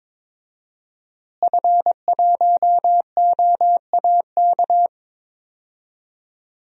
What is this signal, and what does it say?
Morse code, heard as a keyed tone, about 22 wpm, 700 Hz. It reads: F1OAK
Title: DXCC-100 Call Signs - 5 Characters - Repeated 3x - 22wpm